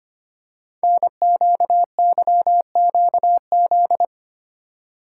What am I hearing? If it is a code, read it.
Morse code, heard as a keyed tone, about 25 wpm, 700 Hz. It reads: NQYQZ
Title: Code Groups - 5 Letters - 25wpm